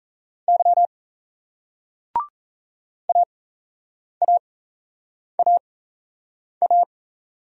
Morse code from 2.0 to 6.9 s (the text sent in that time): AAAA